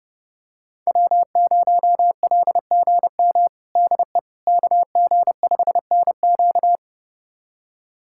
Morse code, heard as a keyed tone, about 32 wpm, 700 Hz. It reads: W0LGM DE KG5NQ
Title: USA Callsign DE Callsign - 5 Characters - 30wpm